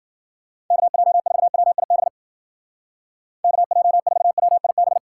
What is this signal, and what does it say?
Morse code, heard as a keyed tone, about 48 wpm, 700 Hz. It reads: KJ4WIB KJ4WIB
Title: Speed Racing - USA 2x3 Call Signs 40wpm